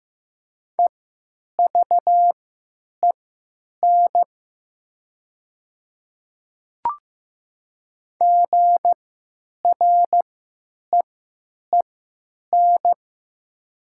Morse code, heard as a keyed tone, about 15 wpm, spaced as for 8 wpm, 700 Hz. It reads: EVEN GREEN
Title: 3x Character Spacing - Top 500 Words - 15wpm